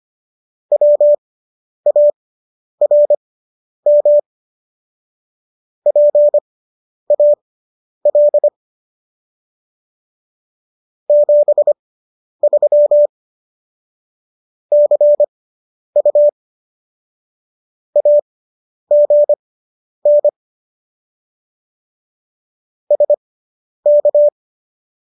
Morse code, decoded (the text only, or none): WARM PAL 73 CU AGN SK